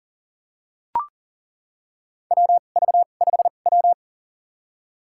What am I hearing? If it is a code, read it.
Morse code, heard as a keyed tone, about 40 wpm, spaced as for 29 wpm, 700 Hz. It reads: WV5W
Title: USA Call Signs - 4 Characters - 2x Character Spacing - 40wpm